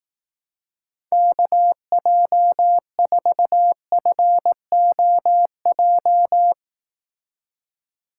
Morse code, heard as a keyed tone, about 18 wpm, 700 Hz. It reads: KJ4FOJ
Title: Speed Racing - USA 2x3 Call Signs 15wpm